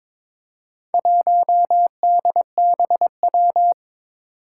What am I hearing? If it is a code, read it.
Morse code, heard as a keyed tone, about 22 wpm, 700 Hz. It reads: 1DBW
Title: Four Challenging Letter-Number Combinations 22wpm